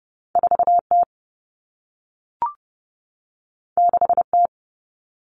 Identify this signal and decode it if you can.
Morse code, heard as a keyed tone, about 30 wpm, 700 Hz. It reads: 4T 6T
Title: All Cut Numbers - 2-Digit 30wpm